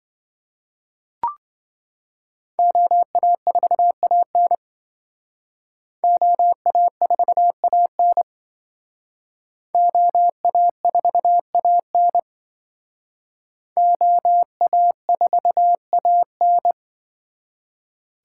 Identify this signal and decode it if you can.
Morse code, heard as a keyed tone, about 26 wpm, 700 Hz. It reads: OA4AN OA4AN OA4AN OA4AN